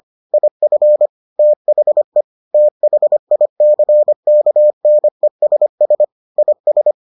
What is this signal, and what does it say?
Morse code, heard as a keyed tone, about 25 wpm, 600 Hz. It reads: IF THE THICKNESS IS